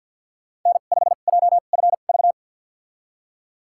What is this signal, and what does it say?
Morse code, heard as a keyed tone, about 52 wpm, spaced as for 37 wpm, 700 Hz. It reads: N5JFV
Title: USA Call Signs - 5 Characters - 2x Character Spacing - 50wpm